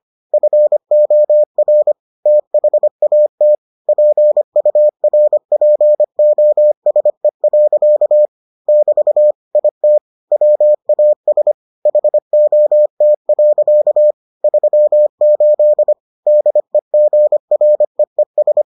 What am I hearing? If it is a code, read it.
Morse code, heard as a keyed tone, about 25 wpm, 600 Hz. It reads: FOR THAT PURPOSE. <BT> IT WAS HOT. 38 DEGREES